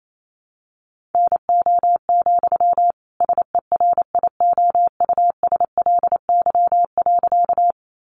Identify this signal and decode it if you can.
Morse code, heard as a keyed tone, about 28 wpm, 700 Hz. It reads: NO, SERIOUSLY.
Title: Sentences from Top 1200 Words 28wpm